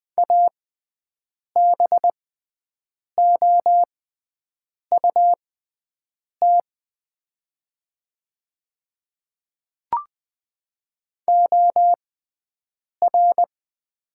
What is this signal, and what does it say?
Morse code, heard as a keyed tone, about 20 wpm, spaced as for 7 wpm, 700 Hz. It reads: ABOUT OR
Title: 6x Character Spacing - Top 100 Words - 20wpm